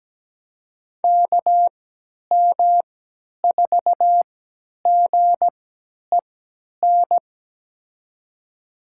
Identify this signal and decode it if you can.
Morse code, heard as a keyed tone, about 17 wpm, spaced as for 10 wpm, 700 Hz. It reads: KM4GEN
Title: USA Call Signs - 6 Characters - 3x Character Spacing - 17wpm